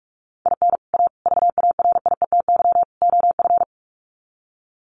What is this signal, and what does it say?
Morse code, heard as a keyed tone, about 44 wpm, 700 Hz. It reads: IN A VARIETY OF
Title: Sets of 4 Words 45wpm